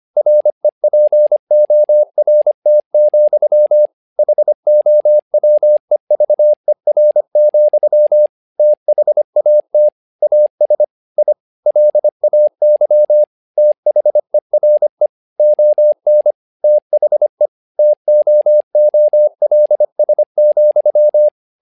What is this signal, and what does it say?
Morse code, heard as a keyed tone, about 25 wpm, 600 Hz. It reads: REPORT, HOWEVER, THAT AS I LAY THERE ON THE TOOLS,
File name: learning center / tower trouble / tower trouble 25wpm